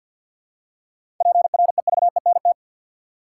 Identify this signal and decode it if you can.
Morse code, heard as a keyed tone, about 52 wpm, 700 Hz. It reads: PREVENT
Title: Seven Letter Words - 50wpm